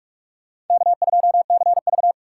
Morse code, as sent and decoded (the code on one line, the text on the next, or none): -.- ..--- -.-. ...-
K2CV